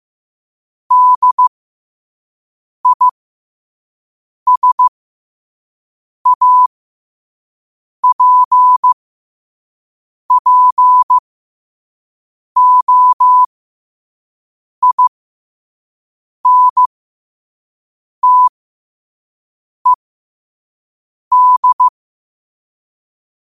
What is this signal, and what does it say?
Morse code, heard as a keyed tone, about 15 wpm, spaced as for 5 wpm, 1000 Hz. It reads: DISAPPOINTED
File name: qotdfiles / 5wpm